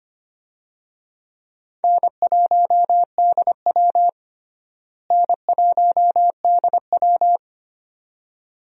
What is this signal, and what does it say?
Morse code, heard as a keyed tone, about 25 wpm, 700 Hz. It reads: N1DW N1DW
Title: USA Call Signs - 4 Characters - Repeated 2x - 25wpm